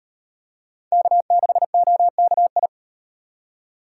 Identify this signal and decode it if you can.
Morse code, heard as a keyed tone, about 38 wpm, 700 Hz. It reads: K6OKI